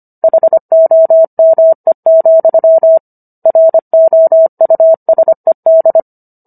Morse code, heard as a keyed tone, about 25 wpm, 650 Hz. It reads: HOME, ROUSED